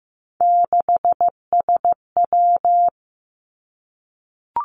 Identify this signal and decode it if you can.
Morse code, heard as a keyed tone, about 15 wpm, 700 Hz. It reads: 6SW
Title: Three Challenging Letter-Number Combinations 15wpm